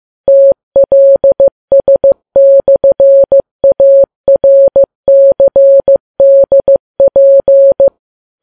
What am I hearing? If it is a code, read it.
Morse code, heard as a keyed tone, about 15 wpm, 550 Hz. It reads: TLS/ARCDP
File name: Files Web tls / Morse on9fv / morse15kps